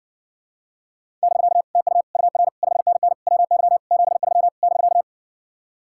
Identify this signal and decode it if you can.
Morse code, heard as a keyed tone, about 60 wpm, 700 Hz. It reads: <BK> TU UR 5NN WY 73 <BK>